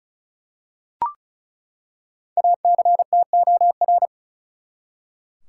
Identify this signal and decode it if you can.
Morse code, heard as a keyed tone, about 36 wpm, 700 Hz. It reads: ACTOR
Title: Five Letter Words - 35wpm